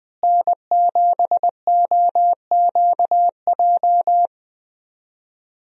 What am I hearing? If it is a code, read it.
Morse code, heard as a keyed tone, about 20 wpm, 700 Hz. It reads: N7OQJ
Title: USA Call Signs - 5 Characters - Repeated 3x - 20wpm